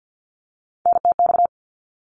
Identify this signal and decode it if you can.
Morse code, heard as a keyed tone, about 50 wpm, 700 Hz. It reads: NTX
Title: ARRL Sections 50wpm